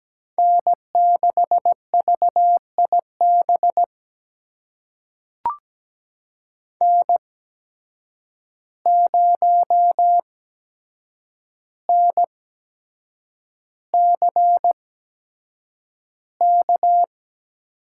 Morse code, heard as a keyed tone, about 17 wpm, spaced as for 7 wpm, 700 Hz. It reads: N6VIB N0NCK
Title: USA Call Signs - 5 Characters - 8x Character Spacing - 17wpm